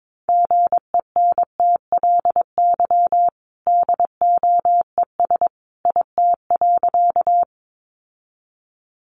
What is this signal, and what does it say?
Morse code, heard as a keyed tone, about 22 wpm, 700 Hz. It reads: GENTLY DOES IT.
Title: Sentences from Top 1500 Words 22wpm